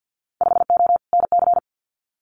Morse code, heard as a keyed tone, about 50 wpm, 700 Hz. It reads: 5O NC
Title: Winter Field Day Exchanges 50wpm